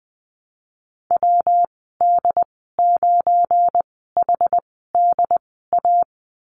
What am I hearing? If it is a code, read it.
Morse code, heard as a keyed tone, about 20 wpm, spaced as for 14 wpm, 700 Hz. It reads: WD9HDA
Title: 173 Lesson 28g - Number 9 - Call Signs 20wpm